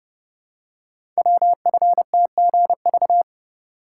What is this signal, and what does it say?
Morse code, heard as a keyed tone, about 30 wpm, 700 Hz. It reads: WFTGV
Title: Code Groups - 5 Letters - 30wpm